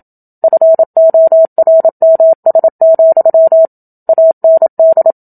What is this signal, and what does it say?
Morse code, heard as a keyed tone, about 27 wpm, 650 Hz. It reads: FORMS, AND
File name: Book Two / 01-Chapter 1 - 27 WPM